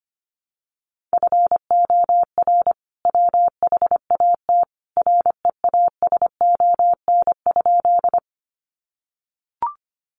Morse code, heard as a keyed tone, about 25 wpm, 700 Hz. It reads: FOR WHAT REASON?
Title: Sentences from Top 700 Words 25wpm